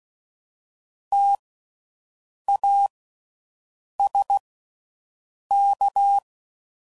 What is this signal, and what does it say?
Morse code, heard as a keyed tone, about 16 wpm, spaced as for 6 wpm, 800 Hz. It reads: TASK
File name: Suffixes / Special INGs / Slow With Faster Ings